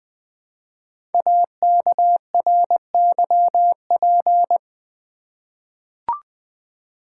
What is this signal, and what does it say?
Morse code, heard as a keyed tone, about 20 wpm, 700 Hz. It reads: AKRYP
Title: Code Groups - 5 Letters - 20wpm